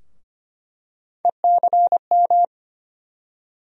Morse code, heard as a keyed tone, about 25 wpm, 700 Hz. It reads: ECM
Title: USA Call Sign Postfixes - 3 Letters 25wpm